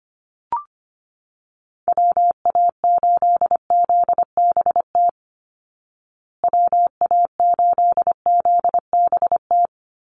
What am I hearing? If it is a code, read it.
Morse code, heard as a keyed tone, about 25 wpm, 700 Hz. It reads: WA8ZBT WA8ZBT
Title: Call Signs Repeated 3x - 25wpm